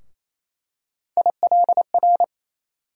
Morse code, heard as a keyed tone, about 28 wpm, 700 Hz. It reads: ILR
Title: USA Call Sign Postfixes - 3 Letters 28wpm